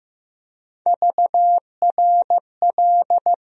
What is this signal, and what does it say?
Morse code, heard as a keyed tone, about 15 wpm, 700 Hz. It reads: VRL
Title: USA Call Sign Postfixes - 3 Letters 15wpm